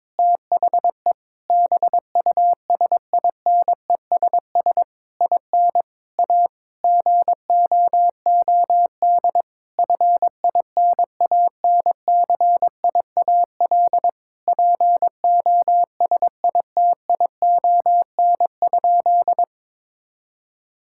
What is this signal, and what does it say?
Morse code, heard as a keyed tone, about 22 wpm, 700 Hz. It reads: THE BUSINESS IN A GOOD FINANCIAL POSITION?
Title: Sentences from Top 1000 Words - Review Part 5 - 22wpm